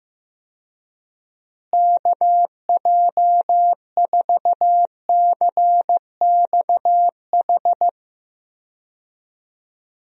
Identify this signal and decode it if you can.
Morse code, heard as a keyed tone, about 15 wpm, 700 Hz. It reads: KJ4CXH
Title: USA Call Signs - 6 Characters - Repeated 3x - 15wpm